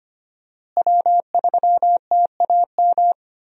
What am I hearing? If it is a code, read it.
Morse code, heard as a keyed tone, about 25 wpm, 700 Hz. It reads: W3TAM
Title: USA Call Signs - 5 Characters - Repeated 3x - 25wpm